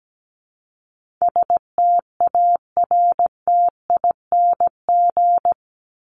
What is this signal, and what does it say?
Morse code, heard as a keyed tone, about 17 wpm, 700 Hz. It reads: STARTING
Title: ING Suffix - Top 100 Words with Base Words 17wpm